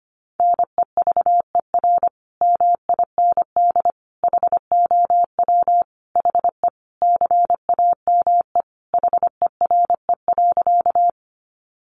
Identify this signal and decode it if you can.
Morse code, heard as a keyed tone, about 25 wpm, 700 Hz. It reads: NEVER MIND HOW HE CAME HERE.
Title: Sentences from Top 500 Words 25wpm